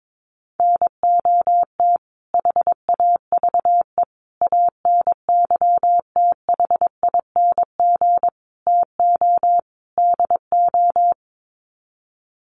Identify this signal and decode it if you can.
Morse code, heard as a keyed tone, about 22 wpm, 700 Hz. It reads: NOT HAVE ANYTHING TO DO
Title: Sets of 5 Words Encore 22wpm